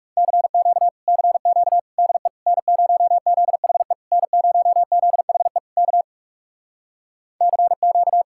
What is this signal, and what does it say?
Morse code, heard as a keyed tone, about 40 wpm, 700 Hz. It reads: CQ CQ DE N0ZHE N0ZHE K CQ